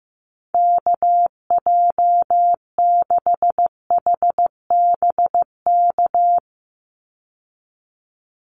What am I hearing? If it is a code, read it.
Morse code, heard as a keyed tone, about 15 wpm, 700 Hz. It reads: KJ6HBK
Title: USA Call Signs 15wpm